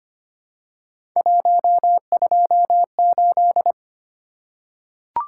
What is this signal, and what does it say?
Morse code, heard as a keyed tone, about 25 wpm, 700 Hz. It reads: 128
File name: Three Letter-Number Combinations - 25wpm